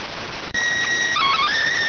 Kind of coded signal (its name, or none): SSTV (slow-scan television)